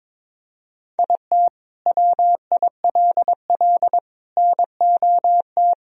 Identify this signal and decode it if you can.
Morse code, heard as a keyed tone, about 22 wpm, 700 Hz. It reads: IT WILL NOT
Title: Sentences from Top 1000 Words - Review Part 1 - 22wpm